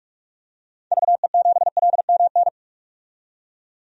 Morse code, heard as a keyed tone, about 46 wpm, 700 Hz. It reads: VE7LMN
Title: DXCC-100 Call Signs - 6 Characters - 45wpm